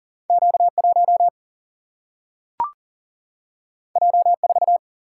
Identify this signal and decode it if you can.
Morse code, heard as a keyed tone, about 40 wpm, 700 Hz. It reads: Q1 J4
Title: Single-Letter Single-Number 40wpm